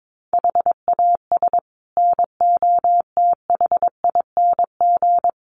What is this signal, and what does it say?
Morse code, heard as a keyed tone, about 22 wpm, 700 Hz. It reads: HAS NOTHING
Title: Sets of 5 Words 22wpm